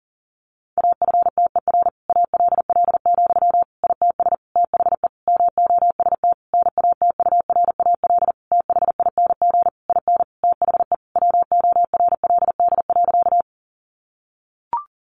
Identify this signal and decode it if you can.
Morse code, heard as a keyed tone, about 40 wpm, 700 Hz. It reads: AFTER ALL, ITS THE MOST NATURAL THING IN THE WORLD.
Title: Sentences from Top 700 Words 40wpm